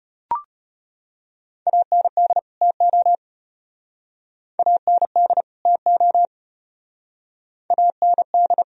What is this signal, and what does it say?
Morse code, heard as a keyed tone, about 30 wpm, 700 Hz. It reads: AND TO AND TO AND
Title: Speed-Racing - Sets of 2 Words 25wpm